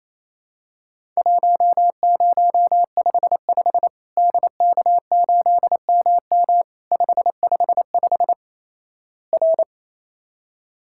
Morse code, heard as a keyed tone, about 28 wpm, 700 Hz. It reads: 1055 DK8MM 555 R
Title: Worked All Europe DX Contest - QTC Exchange - 28wpm